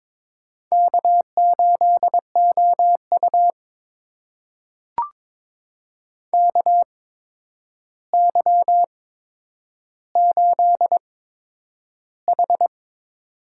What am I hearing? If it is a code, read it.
Morse code, heard as a keyed tone, about 22 wpm, spaced as for 9 wpm, 700 Hz. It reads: K8OU KY8H
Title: USA Call Signs - 4 Characters - 8x Character Spacing - 22wpm